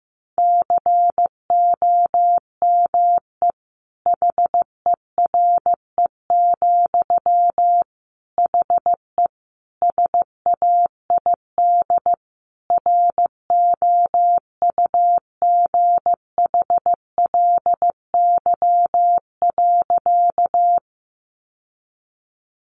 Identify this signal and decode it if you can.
Morse code, heard as a keyed tone, about 15 wpm, 700 Hz. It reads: COME HERE, HE SAID ROUGHLY.
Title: Sentences from Top 1500 Words 15wpm